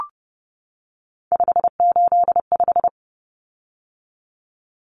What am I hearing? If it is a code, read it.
Morse code, heard as a keyed tone, about 30 wpm, 700 Hz. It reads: 585
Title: Common Cut Numbers - 3-Digit 30wpm